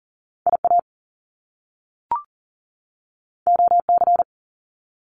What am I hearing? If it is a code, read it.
Morse code, heard as a keyed tone, about 39 wpm, 700 Hz. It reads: IA OC
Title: Top Two Letter Combinations in Words 40wpm